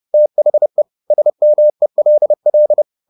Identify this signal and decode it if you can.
Morse code, heard as a keyed tone, about 30 wpm, 600 Hz. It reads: THE SMELL
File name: the old man and the sea / the old man and the sea 1 30wpm